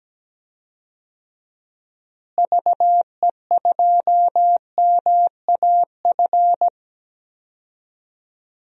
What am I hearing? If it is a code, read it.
Morse code, heard as a keyed tone, about 17 wpm, 700 Hz. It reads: VE2MAF
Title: DXCC-100 Call Signs - 6 Characters - Repeated 3x - 17wpm